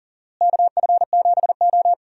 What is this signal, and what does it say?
Morse code, heard as a keyed tone, about 37 wpm, 700 Hz. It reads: KF7O
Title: USA Call Signs - 4 Characters - Repeated 3x - 40wpm